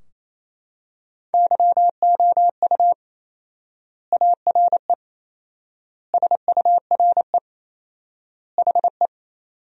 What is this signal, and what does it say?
Morse code, heard as a keyed tone, about 28 wpm, 700 Hz. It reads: YOU ARE SURE HE
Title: 4x Word-Spacing - Sentences from Top 300 Words - 28wpm